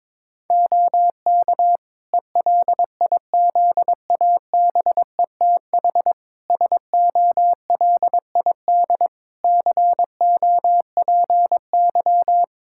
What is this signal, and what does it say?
Morse code, heard as a keyed tone, about 22 wpm, 700 Hz. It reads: OK ELIZABETH SOLID COPY
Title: Practice QSO - Exchange 3 - More Info - 22wpm